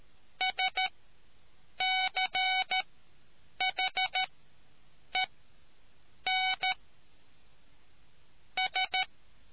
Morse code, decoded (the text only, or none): SCHE N S